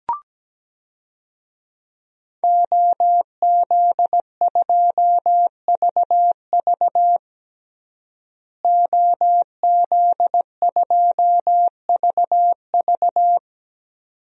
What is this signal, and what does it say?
Morse code, heard as a keyed tone, about 17 wpm, 700 Hz. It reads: OZ2VV OZ2VV
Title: DXCC-100 Call Signs - 5 Characters - Repeated 2x - 17wpm